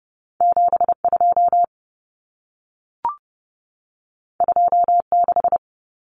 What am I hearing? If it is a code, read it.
Morse code, heard as a keyed tone, about 30 wpm, 700 Hz. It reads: 72 26
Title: Numbers - 2-Digit 30wpm